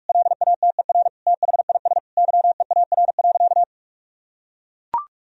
Morse code, heard as a keyed tone, about 42 wpm, 700 Hz. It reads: LATER THIS YEAR.
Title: Sentences from Top 1000 Words - Review Part 3 - 45wpm